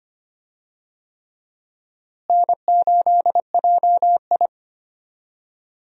Morse code, heard as a keyed tone, about 25 wpm, 700 Hz. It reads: N8JI
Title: USA Call Signs - 4 Characters - Repeated 3x - 25wpm